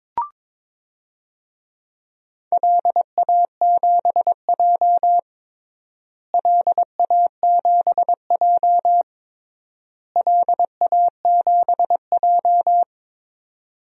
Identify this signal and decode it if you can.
Morse code, heard as a keyed tone, about 22 wpm, 700 Hz. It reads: LA7J LA7J LA7J
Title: DXCC-100 Call Signs - 4 Characters - Repeated 3x - 22wpm